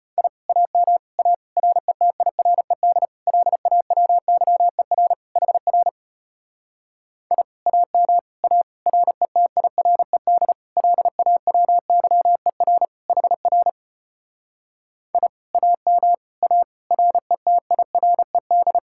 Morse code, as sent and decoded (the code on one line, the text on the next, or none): .. / .- -- / .- / .-. - .. .-. -.. / .-.. .- .-- -.-- . .-. / .... .-. / .. / .- -- / .- / .-. - .. .-. -.. / .-.. .- .-- -.-- . .-. / .... .-. / .. / .- -- / .- / .-. - .. .-. -..
I AM A RETIRED LAWYER HR I AM A RETIRED LAWYER HR I AM A RETIRED